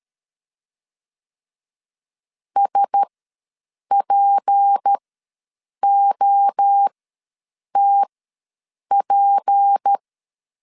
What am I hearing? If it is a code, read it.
Morse code, heard as a keyed tone, about 13 wpm, spaced as for 7 wpm, 800 Hz. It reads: SPOTP
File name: LEZIONI / 6 PKX0 / 6 P ws 12wpm 3117